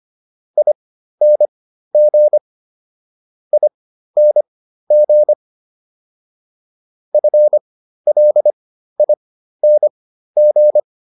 Morse code, heard as a keyed tone, about 25 wpm, spaced as for 13 wpm, 600 Hz. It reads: ING ING FLING
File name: ING5 13